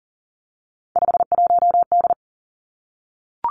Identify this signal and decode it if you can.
Morse code, heard as a keyed tone, about 40 wpm, 700 Hz. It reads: 51D
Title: Three Challenging Letter-Number Combinations 40wpm